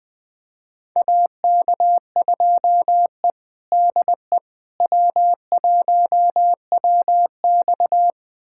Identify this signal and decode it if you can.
Morse code, heard as a keyed tone, about 20 wpm, 700 Hz. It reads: AK2E DE W1WX
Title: USA Callsign DE Callsign - 4 Characters - 20wpm